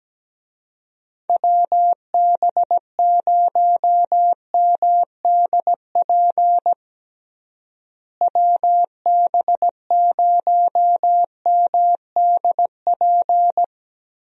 Morse code, heard as a keyed tone, about 17 wpm, 700 Hz. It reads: WB0MDP WB0MDP
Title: USA Call Signs - 6 Characters - Repeated 3x - 17wpm